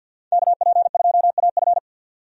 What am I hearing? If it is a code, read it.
Morse code, heard as a keyed tone, about 46 wpm, 700 Hz. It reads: KP2AF
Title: USA Call Signs - 5 Characters - 50wpm